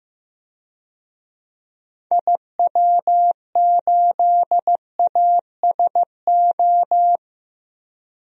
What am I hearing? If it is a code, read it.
Morse code, heard as a keyed tone, about 16 wpm, 700 Hz. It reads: IW8ASO
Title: DXCC-100 Call Signs - 6 Characters - Repeated 2x - 15wpm